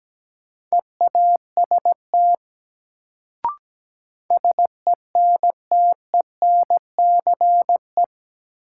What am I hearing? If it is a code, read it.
Morse code, heard as a keyed tone, about 17 wpm, 700 Hz. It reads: EAST SENTENCE
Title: Sentences from Top 1200 Words - New Word Drill Alt - 17wpm